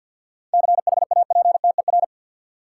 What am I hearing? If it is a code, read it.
Morse code, heard as a keyed tone, about 50 wpm, 700 Hz. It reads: CHAPTER
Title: Seven Letter Words - 50wpm